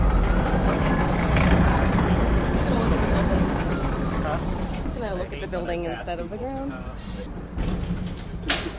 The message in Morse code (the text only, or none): none